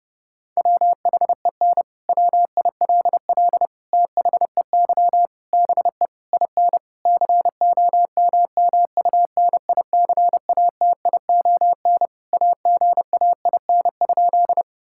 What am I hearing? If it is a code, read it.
Morse code, heard as a keyed tone, about 30 wpm, 700 Hz. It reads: WHEN WILL THEY BE IN COMMUNICATION AGAIN?
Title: Sentences from Top 1000 Words - Review Part 6 - 30wpm